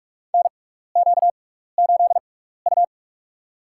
Morse code, heard as a keyed tone, about 43 wpm, spaced as for 15 wpm, 700 Hz. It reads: NQ8U
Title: USA Call Signs - 4 Characters - 6x Character Spacing - 45wpm